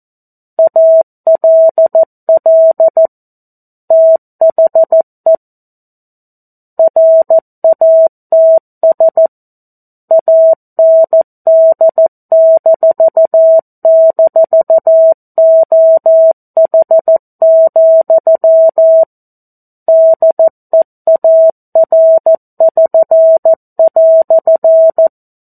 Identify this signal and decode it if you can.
Morse code, heard as a keyed tone, about 14 wpm, 650 Hz. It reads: ALL THE RATS AND--OH, DEAR<SN>"
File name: 02-Chapter-02 14 WPM